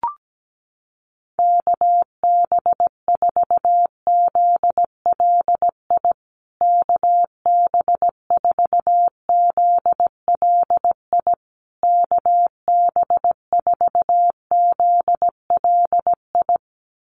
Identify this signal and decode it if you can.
Morse code, heard as a keyed tone, about 17 wpm, 700 Hz. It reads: KB4ZLI KB4ZLI KB4ZLI
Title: USA Call Signs Repeated 3x 17wpm